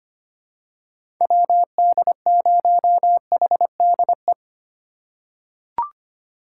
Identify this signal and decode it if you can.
Morse code, heard as a keyed tone, about 25 wpm, 700 Hz. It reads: WD0HDE
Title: USA Call Signs - 6 Characters - 6x Character Spacing - 25wpm